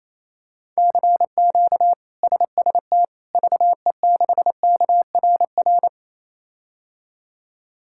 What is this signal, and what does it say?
Morse code, heard as a keyed tone, about 28 wpm, 700 Hz. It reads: CQ SST VE6KRR
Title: K1USN SST QSO Practice - 28wpm